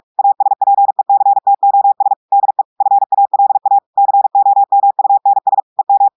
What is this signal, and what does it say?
Morse code, heard as a keyed tone, about 46 wpm, 800 Hz. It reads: ASPECTOS DE FALA COMUNS EM
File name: audios / portugais / portugais texte en clair 2 vitesse 45 mots minutes